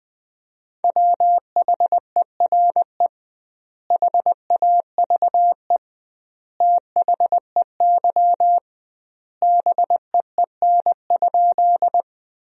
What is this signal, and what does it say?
Morse code, heard as a keyed tone, about 21 wpm, 700 Hz. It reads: WHERE HAVE THEY BEEN?
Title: 2x Word-Spacing - Sentences from Top 200 Words - 20wpm